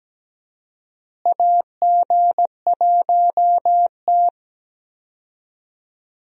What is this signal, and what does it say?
Morse code, heard as a keyed tone, about 17 wpm, 700 Hz. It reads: AG1T
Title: USA Call Signs - 4 Characters - Repeated 3x - 17wpm